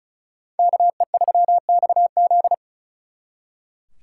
Morse code, heard as a keyed tone, about 45 wpm, 700 Hz. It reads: KE3XZ